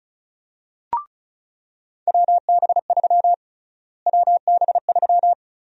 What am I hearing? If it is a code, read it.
Morse code, heard as a keyed tone, about 34 wpm, 700 Hz. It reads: WB3 WB3
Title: USA Call Sign Prefixes - Repeated 2x - 35wpm